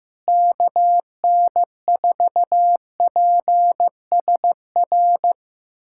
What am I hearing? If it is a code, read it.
Morse code, heard as a keyed tone, about 15 wpm, 700 Hz. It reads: KN4PSR
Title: USA Callsign DE Callsign - 6 Characters - 15wpm